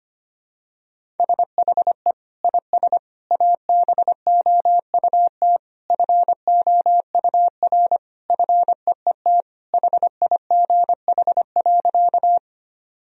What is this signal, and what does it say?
Morse code, heard as a keyed tone, about 25 wpm, 700 Hz. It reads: SHE IS ABOUT FOUR FEET HIGH.
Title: Sentences from Top 1000 Words - Review Part 1 - 25wpm